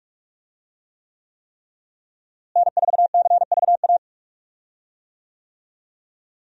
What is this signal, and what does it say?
Morse code, heard as a keyed tone, about 46 wpm, 700 Hz. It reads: N4CVA
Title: USA Call Signs - 5 Characters - Repeated 3x - 45wpm